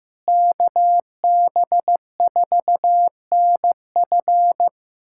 Morse code, heard as a keyed tone, about 15 wpm, 700 Hz. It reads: KB4NF
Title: USA Callsign DE Callsign - 5 Characters - 15wpm